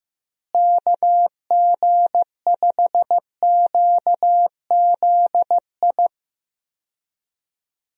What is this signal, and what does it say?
Morse code, heard as a keyed tone, about 15 wpm, 700 Hz. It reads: KG5QZI
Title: USA Call Signs - 6 Characters - Repeated 3x - 15wpm